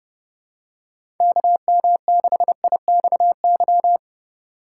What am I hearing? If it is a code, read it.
Morse code, heard as a keyed tone, about 32 wpm, 700 Hz. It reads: KM6IXY
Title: USA Call Signs - 4x Character Spacing - 30wpm